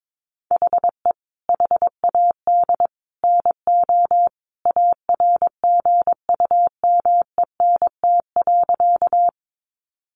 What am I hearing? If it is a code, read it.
Morse code, heard as a keyed tone, about 22 wpm, 700 Hz. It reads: HE HAD NO ARGUMENT.
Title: Sentences from Top 1400 Words 22wpm